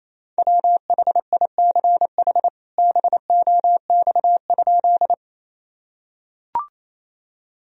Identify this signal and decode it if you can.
Morse code, heard as a keyed tone, about 28 wpm, 700 Hz. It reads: WHICH BOX?
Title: Sentences from Top 1000 Words - Review Part 2 - 28wpm